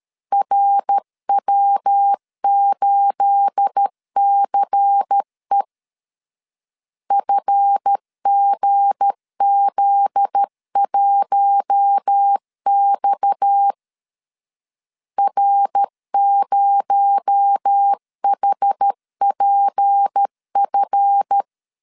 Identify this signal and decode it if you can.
Morse code, heard as a keyed tone, about 13 wpm, 800 Hz. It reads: RW8CE FGZ1X R0HPF
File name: LEZIONI / 9 28 12wpm 2351